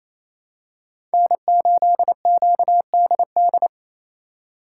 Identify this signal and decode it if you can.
Morse code, heard as a keyed tone, about 28 wpm, 700 Hz. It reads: N8QDD
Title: USA Call Signs - 2x Character Spacing - 28wpm